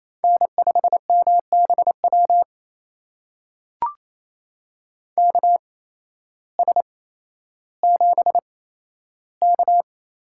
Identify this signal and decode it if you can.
Morse code, heard as a keyed tone, about 28 wpm, spaced as for 13 wpm, 700 Hz. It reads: N5MBW KS7K
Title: USA Call Signs - 8x Character Spacing - 28wpm